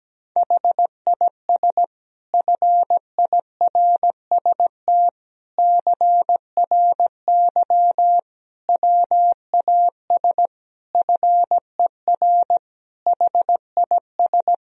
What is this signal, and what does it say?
Morse code, heard as a keyed tone, about 17 wpm, 700 Hz. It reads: HIS FIRST CRY WAS FER HIS
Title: Sentences from Top 500 Words and Abbreviations 17wpm